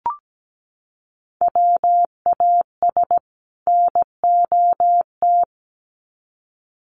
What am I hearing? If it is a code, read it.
Morse code, heard as a keyed tone, about 17 wpm, 700 Hz. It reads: WAS NOT
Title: Sets of 2 Words 17wpm